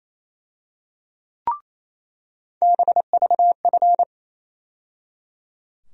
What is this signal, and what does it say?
Morse code, heard as a keyed tone, about 28 wpm, 700 Hz. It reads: BVF
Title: USA Call Sign Postfixes - 3 Letters 28wpm